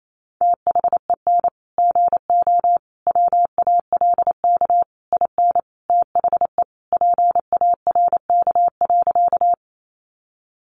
Morse code, heard as a keyed tone, about 28 wpm, 700 Hz. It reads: THEN GO WALK IN THE PARK.
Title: Sentences from Top 1300 Words 28wpm